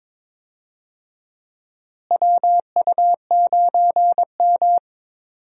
Morse code, heard as a keyed tone, about 21 wpm, 700 Hz. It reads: WU9M
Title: USA Call Signs - 4 Characters - Repeated 2x - 22wpm